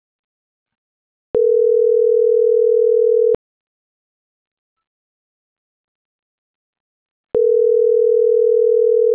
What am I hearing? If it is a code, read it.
Ringback tone (North America), 440 + 480 Hz.